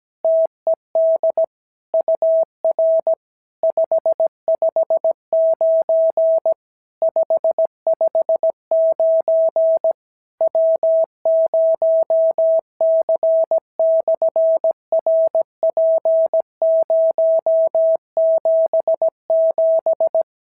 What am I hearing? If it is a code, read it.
Morse code, heard as a keyed tone, about 17 wpm, 650 Hz. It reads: TED UR 559 559 W0C/RP077
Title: SOTA QSO Practice 17wpm